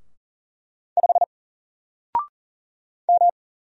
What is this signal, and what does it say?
Morse code, heard as a keyed tone, about 41 wpm, 700 Hz. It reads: 5 M
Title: Single Character (Letter:Number:Punctuation:Prosign) 40wpm